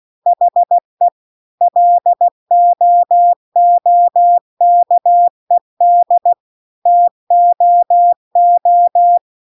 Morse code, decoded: HE LOOKED TOO

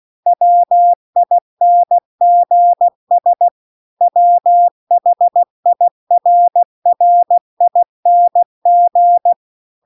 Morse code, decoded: WINGS WHIRRING

